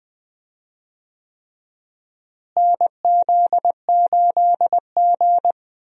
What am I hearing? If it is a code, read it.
Morse code, heard as a keyed tone, about 20 wpm, 700 Hz. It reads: NZ8G
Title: USA Call Signs - 4 Characters - Repeated 3x - 20wpm